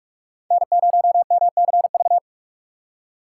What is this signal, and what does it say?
Morse code, heard as a keyed tone, about 44 wpm, 700 Hz. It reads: N0MCV
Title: USA Call Signs - 5 Characters - 2x Character Spacing - 45wpm